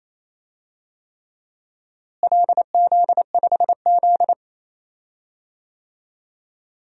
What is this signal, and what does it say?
Morse code, heard as a keyed tone, about 28 wpm, 700 Hz. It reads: LZ5Z
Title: DXCC-100 Call Signs - 4 Characters - Repeated 3x - 28wpm